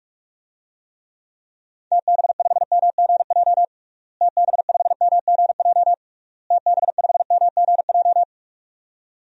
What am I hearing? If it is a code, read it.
Morse code, heard as a keyed tone, about 46 wpm, 700 Hz. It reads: TB5MGJ TB5MGJ TB5MGJ
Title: DXCC-100 Call Signs - 6 Characters - Repeated 3x - 45wpm